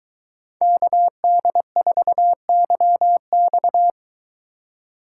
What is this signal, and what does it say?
Morse code, heard as a keyed tone, about 23 wpm, 700 Hz. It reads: KD4YX